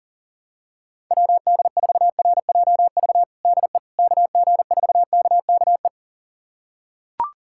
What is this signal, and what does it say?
Morse code, heard as a keyed tone, about 37 wpm, 700 Hz. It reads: WD4RJV DE KG4KKE
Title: USA Callsign DE Callsign - 6 Characters - 40wpm